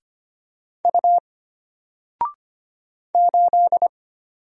Morse code, heard as a keyed tone, about 25 wpm, 700 Hz. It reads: U 8
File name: Single-Letter-Number-25wpm